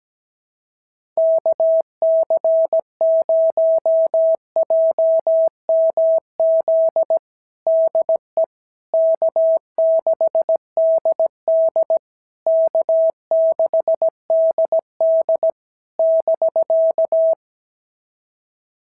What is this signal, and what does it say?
Morse code, heard as a keyed tone, about 18 wpm, 650 Hz. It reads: KC0JMZ DE K6DD K6DD <BK>
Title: SOTA QSO Practice 17wpm